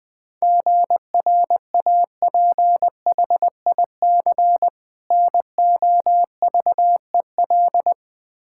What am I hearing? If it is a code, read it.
Morse code, heard as a keyed tone, about 20 wpm, 700 Hz. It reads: GRAPHIC NOVEL
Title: Difficult Words with Hints - 20wpm